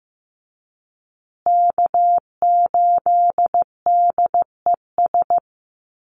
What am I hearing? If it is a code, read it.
Morse code, heard as a keyed tone, about 15 wpm, 700 Hz. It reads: K8DES
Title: USA Call Signs Repeated 3x 15wpm